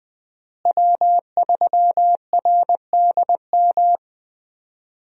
Morse code, heard as a keyed tone, about 20 wpm, 700 Hz. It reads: W3RDM
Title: USA Call Signs - 5 Characters - Repeated 2x - 20wpm